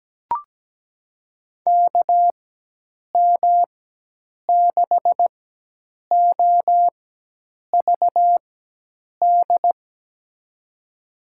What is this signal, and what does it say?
Morse code, heard as a keyed tone, about 17 wpm, spaced as for 8 wpm, 700 Hz. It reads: KM6OVD
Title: USA Call Signs - 6 Characters - 4x Character Spacing - 17wpm